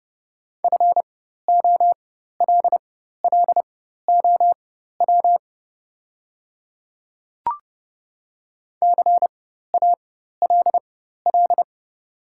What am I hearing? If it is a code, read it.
Morse code, heard as a keyed tone, about 30 wpm, spaced as for 14 wpm, 700 Hz. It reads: FOLLOW CALL
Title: 4x Character Spacing - Top 500 Words - 30wpm